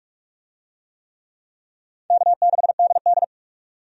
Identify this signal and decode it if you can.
Morse code, heard as a keyed tone, about 46 wpm, 700 Hz. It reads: K6DD
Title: USA Call Signs - 4 Characters - 45wpm